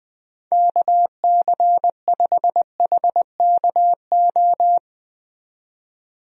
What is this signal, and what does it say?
Morse code, heard as a keyed tone, about 20 wpm, 700 Hz. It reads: KC5HKO
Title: USA Call Signs - 6 Characters - 20wpm